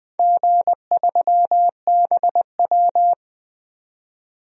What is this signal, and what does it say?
Morse code, heard as a keyed tone, about 20 wpm, 700 Hz. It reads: G3BW
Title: DXCC-100 Call Signs - 4 Characters - 20wpm